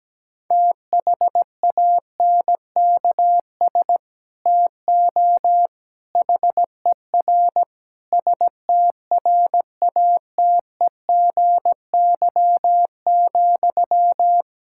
Morse code, heard as a keyed tone, about 17 wpm, 700 Hz. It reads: THANKS TO HER STRATEGY,
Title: Sentences from Top 1000 Words - Review Part 5 - 17wpm